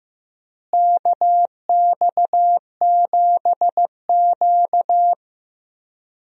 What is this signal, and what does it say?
Morse code, heard as a keyed tone, about 15 wpm, 700 Hz. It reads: KX7Q
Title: USA Call Signs - 4 Characters - Repeated 3x - 15wpm